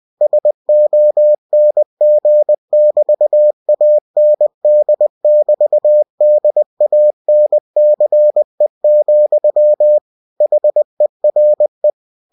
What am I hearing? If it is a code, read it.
Morse code, heard as a keyed tone, about 20 wpm, 600 Hz. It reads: SONG<BT>AND<BT>DANCE, HERE
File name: cw-books / winnie-the-pooh / winnie-20wpm-page-0055